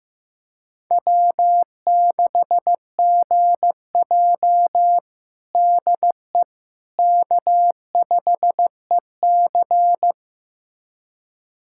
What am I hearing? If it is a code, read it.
Morse code, heard as a keyed tone, about 15 wpm, 700 Hz. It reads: W6GJ DE K5EC
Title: USA Callsign DE Callsign - 4 Characters - 15wpm